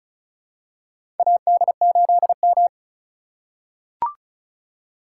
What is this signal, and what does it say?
Morse code, heard as a keyed tone, about 36 wpm, 700 Hz. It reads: AD8M
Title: USA Call Signs - 4 Characters - 4x Character Spacing - 35wpm